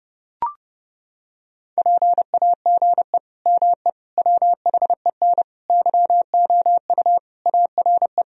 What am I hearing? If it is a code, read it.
Morse code, heard as a keyed tone, about 30 wpm, 700 Hz. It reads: PAGE ME WHEN YOU ARE
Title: Sentences from Top 1000 Words - Review Part 1 - 30wpm